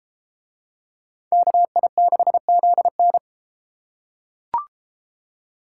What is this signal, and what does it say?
Morse code, heard as a keyed tone, about 31 wpm, 700 Hz. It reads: KI6ZN